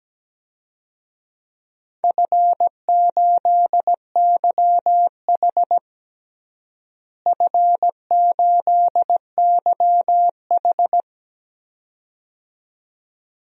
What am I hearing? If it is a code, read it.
Morse code, heard as a keyed tone, about 17 wpm, 700 Hz. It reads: F8YH F8YH
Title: DXCC-100 Call Signs - 4 Characters - Repeated 2x - 17wpm